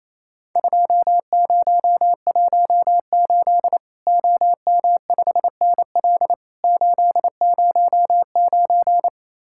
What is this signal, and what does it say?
Morse code, heard as a keyed tone, about 28 wpm, 700 Hz. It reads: 2018 OM5NL 809
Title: Worked All Europe DX Contest - QTC Exchange - 28wpm